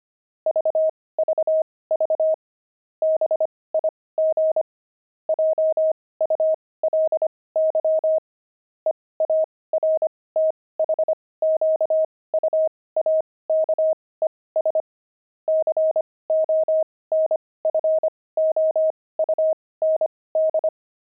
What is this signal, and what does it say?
Morse code, heard as a keyed tone, about 25 wpm, spaced as for 18 wpm, 630 Hz. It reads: VVV BIG JULY EARTHQUAKES CONFOUND
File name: pangram-fs-18-cs-25-zany